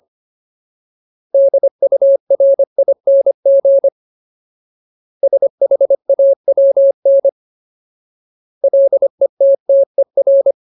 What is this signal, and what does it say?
Morse code, heard as a keyed tone, about 25 wpm, 550 Hz. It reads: DURING SHAWN LETTER
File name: Ultimate Word List / 25WPM Ultimate